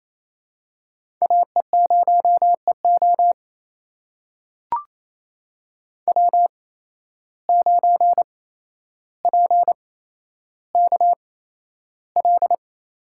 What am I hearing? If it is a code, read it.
Morse code, heard as a keyed tone, about 28 wpm, spaced as for 12 wpm, 700 Hz. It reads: AE0EO W9PKL
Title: USA Call Signs - 5 Characters - 8x Character Spacing - 28wpm